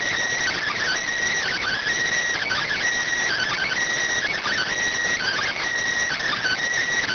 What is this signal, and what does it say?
Emergency Alert System (SAME) tones (header bursts), heard as harsh buzzing bursts. It is heard from 0.5 to 6.4 s.